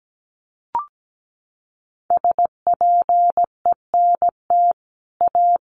Morse code, heard as a keyed tone, about 17 wpm, 700 Hz. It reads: SPENT A
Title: Sets of 5 Words 17wpm